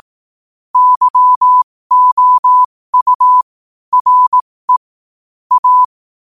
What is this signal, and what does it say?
Morse code, heard as a keyed tone, about 18 wpm, spaced as for 15 wpm, 1000 Hz. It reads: YOU RE A